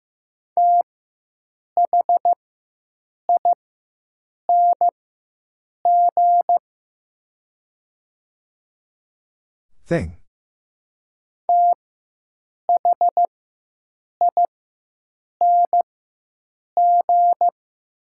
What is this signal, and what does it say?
Morse code, heard as a keyed tone, about 15 wpm, spaced as for 7 wpm, 700 Hz. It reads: THING THING
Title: 4x Character Spacing - Top 300 Words - 15wpm